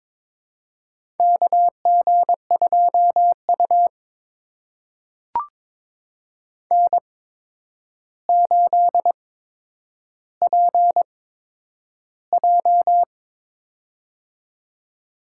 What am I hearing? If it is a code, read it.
Morse code, heard as a keyed tone, about 22 wpm, spaced as for 9 wpm, 700 Hz. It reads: KG2U N8PJ
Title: USA Call Signs - 4 Characters - 8x Character Spacing - 22wpm